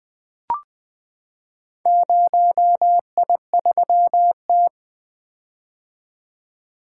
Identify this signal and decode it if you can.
Morse code, heard as a keyed tone, about 20 wpm, 700 Hz. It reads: 0I3T
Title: Four Letter-Number Combinations - 20wpm